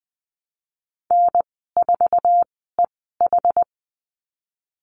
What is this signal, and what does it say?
Morse code, heard as a keyed tone, about 20 wpm, spaced as for 15 wpm, 700 Hz. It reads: N4EH
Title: 088 Lesson 15g - Number 5 - Call Signs 20wpm